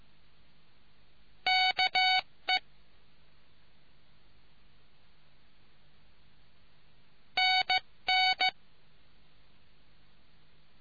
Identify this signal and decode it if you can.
Morse code, heard as a keyed tone, about 15 wpm, 750 Hz. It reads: KE NN